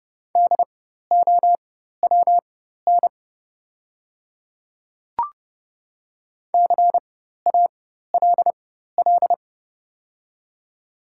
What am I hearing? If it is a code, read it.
Morse code, heard as a keyed tone, about 32 wpm, spaced as for 14 wpm, 700 Hz. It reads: DOWN CALL
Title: 4x Character Spacing - Top 100 Words - 30wpm